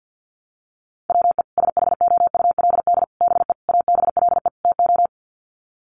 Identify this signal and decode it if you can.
Morse code, heard as a keyed tone, about 50 wpm, 700 Hz. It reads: WE SHOULD BE ABLE TO
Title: Sets of 5 Words Encore2 50wpm